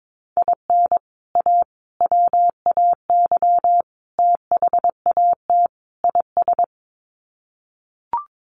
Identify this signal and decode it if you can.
Morse code, heard as a keyed tone, about 22 wpm, 700 Hz. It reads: IN A WAY THAT IS
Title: Sets of 5 Words Encore 22wpm